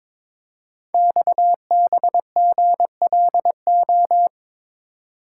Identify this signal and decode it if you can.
Morse code, heard as a keyed tone, about 22 wpm, 700 Hz. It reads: XBGLO
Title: Code Groups - 5 Letter:Numbers - 22wpm